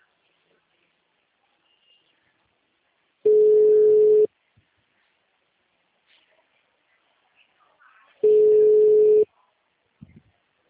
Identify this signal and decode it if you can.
Ringback tone (Europe, 425 Hz).